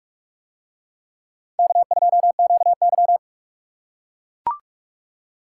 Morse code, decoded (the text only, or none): K2QY